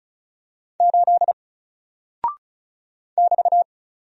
Morse code, heard as a keyed tone, about 36 wpm, 700 Hz. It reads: 8 <BT>
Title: Single Character (Letter:Number:Punctuation:Prosign) Alt 35wpm